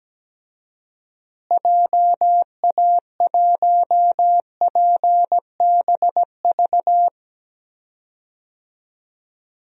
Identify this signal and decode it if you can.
Morse code, heard as a keyed tone, about 17 wpm, 700 Hz. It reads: JA1PBV
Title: DXCC-100 Call Signs - 6 Characters - Repeated 2x - 17wpm